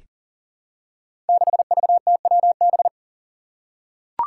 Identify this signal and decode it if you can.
Morse code, heard as a keyed tone, about 42 wpm, 700 Hz. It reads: 6VTWB
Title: Five Challenging Letter-Number Combinations 40wpm